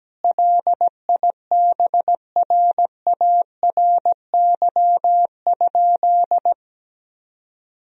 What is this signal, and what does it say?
Morse code, heard as a keyed tone, about 17 wpm, 700 Hz. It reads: LIBRARY?
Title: Sentences from Top 1000 Words - Review Part 4 - 17wpm